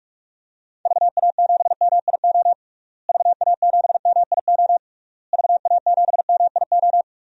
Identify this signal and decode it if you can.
Morse code, heard as a keyed tone, about 46 wpm, 700 Hz. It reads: VA7MIO VA7MIO VA7MIO
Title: DXCC-100 Call Signs - 6 Characters - Repeated 3x - 45wpm